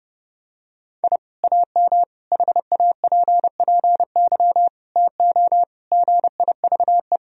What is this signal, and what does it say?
Morse code, heard as a keyed tone, about 30 wpm, 700 Hz. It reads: I AM HAPPY TO GIVE
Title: Sentences from Top 1000 Words - Review Part 3 - 30wpm